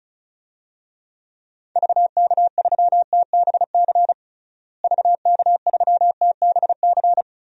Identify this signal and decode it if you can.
Morse code, heard as a keyed tone, about 36 wpm, 700 Hz. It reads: VK3TBC VK3TBC
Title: DXCC-100 Call Signs - 6 Characters - Repeated 3x - 35wpm